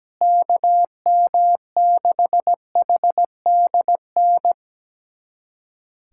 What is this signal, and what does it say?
Morse code, heard as a keyed tone, about 17 wpm, 700 Hz. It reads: KM6HDN
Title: USA Callsign DE Callsign - 6 Characters - 17wpm